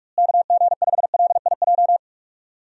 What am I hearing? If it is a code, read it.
Morse code, heard as a keyed tone, about 43 wpm, 700 Hz. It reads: KG5LIJ
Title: USA Call Signs - 6 Characters - Repeated 3x - 45wpm